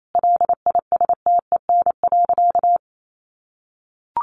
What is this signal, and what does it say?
Morse code, heard as a keyed tone, about 28 wpm, 700 Hz. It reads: LISTEN.
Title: Sentences from Top 1300 Words 28wpm